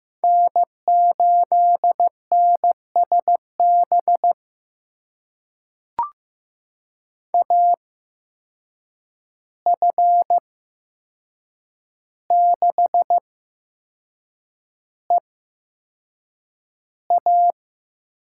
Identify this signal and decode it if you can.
Morse code, heard as a keyed tone, about 15 wpm, spaced as for 6 wpm, 700 Hz. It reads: N8NSB AF6EA
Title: USA Call Signs - 5 Characters - 8x Character Spacing - 15wpm